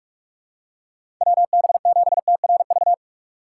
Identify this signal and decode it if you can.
Morse code, heard as a keyed tone, about 44 wpm, 700 Hz. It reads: WB7TRV
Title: USA Call Signs - 6 Characters - 6x Character Spacing - 45wpm